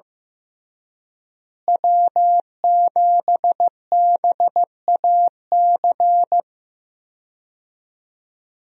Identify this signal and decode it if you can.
Morse code, heard as a keyed tone, about 16 wpm, 700 Hz. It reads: W7BAC
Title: USA Call Signs - 5 Characters - Repeated 3x - 15wpm